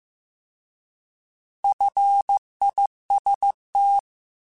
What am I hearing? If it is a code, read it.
Morse code, heard as a keyed tone, about 15 wpm, 800 Hz. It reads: FIST
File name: Alphabetical / Alphabetical mp3s / Fs @13